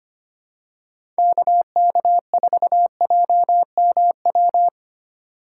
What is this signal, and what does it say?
Morse code, heard as a keyed tone, about 25 wpm, 700 Hz. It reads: KK4JMW
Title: USA Call Signs - 6 Characters - 4x Character Spacing - 25wpm